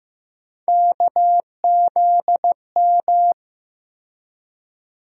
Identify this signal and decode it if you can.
Morse code, heard as a keyed tone, about 15 wpm, 700 Hz. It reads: KZM
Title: USA Call Sign Postfixes - 3 Letters 15wpm